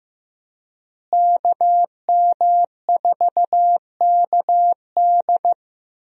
Morse code, heard as a keyed tone, about 15 wpm, 700 Hz. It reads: KM4KD
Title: USA Call Signs - 5 Characters - Repeated 3x - 15wpm